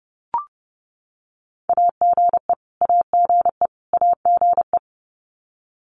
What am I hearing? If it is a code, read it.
Morse code, heard as a keyed tone, about 30 wpm, 700 Hz. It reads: AGE AGE AGE
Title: Top 500 Words Repeated 30wpm